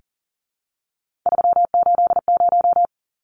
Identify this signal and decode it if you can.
Morse code, heard as a keyed tone, about 40 wpm, 700 Hz. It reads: 380
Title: Numbers - 3-Digit 40wpm